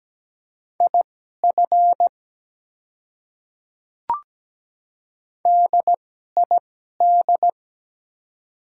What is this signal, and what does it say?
Morse code, heard as a keyed tone, about 17 wpm, spaced as for 12 wpm, 700 Hz. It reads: IF DID
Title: 2x Character Spacing - Top 200 Words - 17wpm